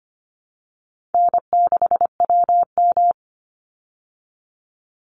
Morse code, heard as a keyed tone, about 25 wpm, 700 Hz. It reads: N6WM
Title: Easy Call Signs Repeated 3x - 25wpm